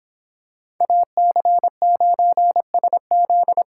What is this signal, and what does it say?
Morse code, heard as a keyed tone, about 24 wpm, 700 Hz. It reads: AC9SZ